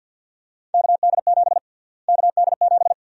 Morse code, heard as a keyed tone, about 51 wpm, 700 Hz. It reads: KD7 KD7
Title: USA Call Sign Prefixes - Repeated 2x - 50wpm